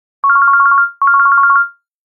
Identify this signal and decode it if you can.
There is no code.